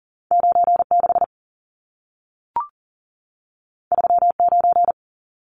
Morse code, heard as a keyed tone, about 40 wpm, 700 Hz. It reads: 96 39
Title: Numbers - 2-Digit 40wpm